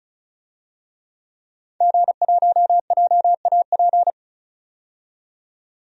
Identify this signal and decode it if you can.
Morse code, heard as a keyed tone, about 36 wpm, 700 Hz. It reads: G1JAP
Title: DXCC-100 Call Signs - 5 Characters - 35wpm